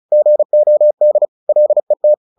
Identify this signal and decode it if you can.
Morse code, heard as a keyed tone, about 34 wpm, 600 Hz. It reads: GOD LET
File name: the old man and the sea / the old man and the sea 1 35wpm